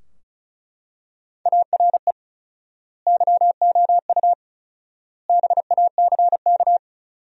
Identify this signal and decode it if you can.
Morse code, heard as a keyed tone, about 36 wpm, 700 Hz. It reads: ARE YOU BACK
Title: 4x Word-Spacing - Sentences from Top 200 Words - 35wpm